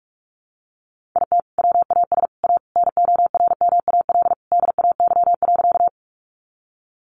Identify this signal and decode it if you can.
Morse code, heard as a keyed tone, about 44 wpm, 700 Hz. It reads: IT WAS A NORMAL DAY.
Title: Sentences from Top 1200 Words 45wpm